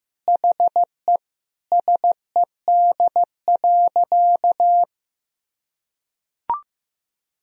Morse code, heard as a keyed tone, about 15 wpm, 700 Hz. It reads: HE SED.
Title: Sentences from Top 200 Words and Abbreviations 15wpm